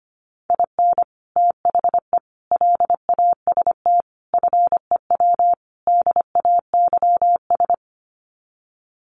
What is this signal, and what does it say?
Morse code, heard as a keyed tone, about 25 wpm, 700 Hz. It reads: IN THE LAST FEW DAYS
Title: Sets of 5 Words Encore2 25wpm